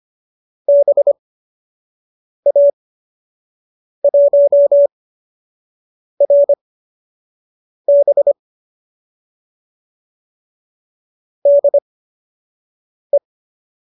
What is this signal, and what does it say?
Morse code, heard as a keyed tone, about 25 wpm, spaced as for 6 wpm, 570 Hz. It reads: BA1RB DE
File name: qso002 06